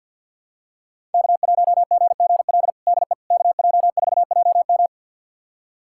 Morse code, heard as a keyed tone, about 50 wpm, 700 Hz. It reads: K1GGL DE KJ4JM
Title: USA Callsign DE Callsign - 5 Characters - 50wpm